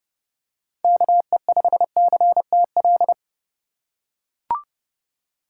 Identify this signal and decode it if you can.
Morse code, heard as a keyed tone, about 32 wpm, 700 Hz. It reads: KE5CTL
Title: USA Call Signs - 6 Characters - 3x Character Spacing - 30wpm